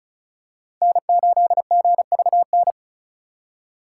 Morse code, heard as a keyed tone, about 36 wpm, 700 Hz. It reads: N8GVN
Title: USA Call Signs - 5 Characters - 3x Character Spacing - 35wpm